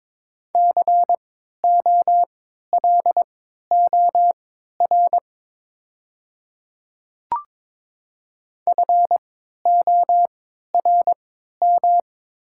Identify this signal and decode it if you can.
Morse code, heard as a keyed tone, about 22 wpm, spaced as for 13 wpm, 700 Hz. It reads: COLOR FORM
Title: 3x Character Spacing - Top 300 Words - 22wpm